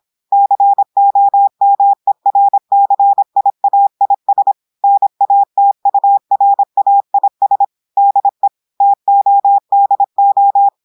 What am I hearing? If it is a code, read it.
Morse code, heard as a keyed tone, about 26 wpm, 800 Hz. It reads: COMERCIAIS NATURAIS DE TODO
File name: portugais texte en clair 2 vitesse 26 mots minutes